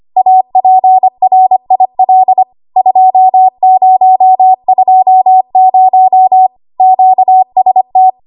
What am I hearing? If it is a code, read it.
Morse code, heard as a keyed tone, about 25 wpm, 750 Hz. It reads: APRIL 2020 QST
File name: Archive / 25 WPM / 25WPM